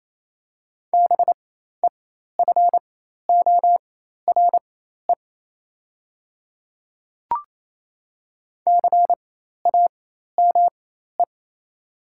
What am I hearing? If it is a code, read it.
Morse code, heard as a keyed tone, about 28 wpm, spaced as for 13 wpm, 690 Hz. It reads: BEFORE CAME
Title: 4x Character Spacing - Top 200 Words - 28wpm